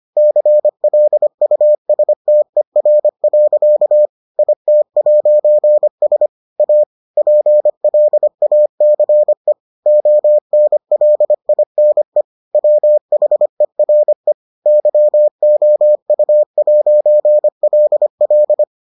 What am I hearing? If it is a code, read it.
Morse code, heard as a keyed tone, about 25 wpm, 600 Hz. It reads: CLUSTER. IT'S A PLACE ONLINE WHERE YOU'LL